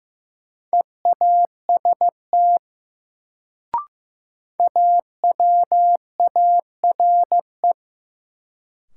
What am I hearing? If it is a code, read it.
Morse code, heard as a keyed tone, about 15 wpm, 700 Hz. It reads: EAST AWARE
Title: Sentences from Top 1200 Words - New Word Drill Alt - 15wpm